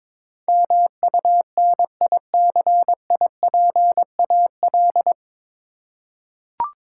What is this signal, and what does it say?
Morse code, heard as a keyed tone, about 22 wpm, 700 Hz. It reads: MUNICIPAL
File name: 9 Letter Words - First 3 Letters Given - 22wpm